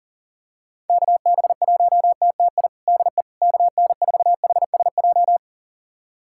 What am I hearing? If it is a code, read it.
Morse code, heard as a keyed tone, about 40 wpm, 700 Hz. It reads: KB1TTI DE KN4HSJ
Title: USA Callsign DE Callsign 40wpm